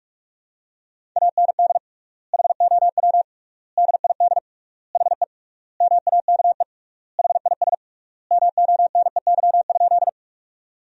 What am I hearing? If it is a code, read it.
Morse code, heard as a keyed tone, about 46 wpm, 700 Hz. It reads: AND HOW DID HE MAKE HIS MONEY?
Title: 3x Word-Spacing - Sentences from Top 400 Words - 45wpm